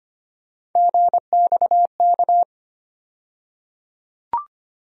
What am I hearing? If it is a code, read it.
Morse code, heard as a keyed tone, about 25 wpm, 700 Hz. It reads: GXK
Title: USA Call Sign Postfixes - 3 Letters 25wpm